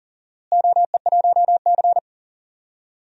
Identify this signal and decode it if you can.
Morse code, heard as a keyed tone, about 37 wpm, 700 Hz. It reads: OE1C
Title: DXCC-100 Call Signs - 4 Characters - Repeated 3x - 40wpm